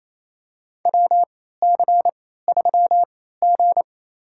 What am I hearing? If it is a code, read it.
Morse code, heard as a keyed tone, about 28 wpm, spaced as for 16 wpm, 700 Hz. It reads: WC3G
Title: USA Call Signs - 4 Characters - 3x Character Spacing - 28wpm